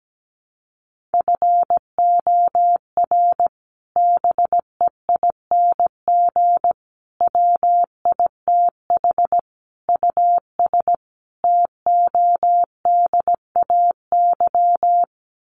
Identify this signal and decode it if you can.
Morse code, heard as a keyed tone, about 17 wpm, 700 Hz. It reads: FOR BEING WITH US TODAY
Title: Sets of 5 Words Encore2 17wpm